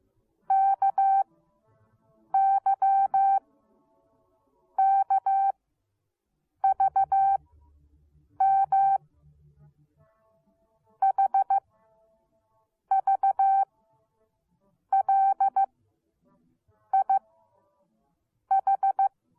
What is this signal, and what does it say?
Morse code, heard as a keyed tone, about 15 wpm, spaced as for 6 wpm, 780 Hz. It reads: KYKVM HVLIH